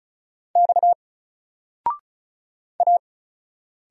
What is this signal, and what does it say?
Morse code, heard as a keyed tone, about 36 wpm, 700 Hz. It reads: X A
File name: Single-Letter-Number-35wpm